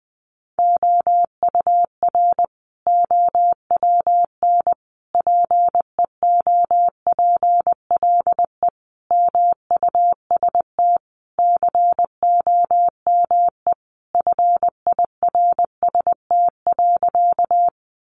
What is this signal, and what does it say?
Morse code, heard as a keyed tone, about 20 wpm, 700 Hz. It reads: OUR OWN PEOPLE MUST COME FIRST.
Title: Sentences from Top 300 Words 20wpm